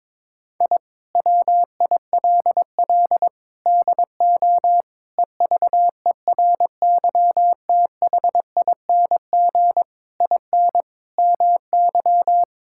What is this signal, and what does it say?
Morse code, heard as a keyed tone, about 22 wpm, 700 Hz. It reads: I WILL DO EVERYTHING IN MY
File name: practice / 22wpm / Sentences with 8-Word Phrases - 22wpm